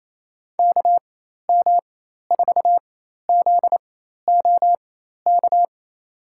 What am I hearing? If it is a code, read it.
Morse code, heard as a keyed tone, about 28 wpm, spaced as for 13 wpm, 700 Hz. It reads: KM4ZOK
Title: USA Call Signs - 4x Character Spacing - 28wpm